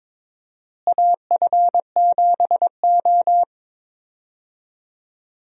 Morse code, heard as a keyed tone, about 22 wpm, 700 Hz. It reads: AF7O
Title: USA Call Signs - 4 Characters - Repeated 2x - 22wpm